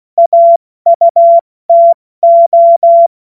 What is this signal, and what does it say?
Morse code, heard as a keyed tone, about 16 wpm, 680 Hz. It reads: AUTO
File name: Morsekurs / Witze / Witz 26. Oktober 2021, 70 F80